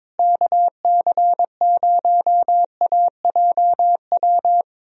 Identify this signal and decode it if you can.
Morse code, heard as a keyed tone, about 22 wpm, 700 Hz. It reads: KC0AJW
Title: USA Call Signs - 6 Characters - Repeated 3x - 22wpm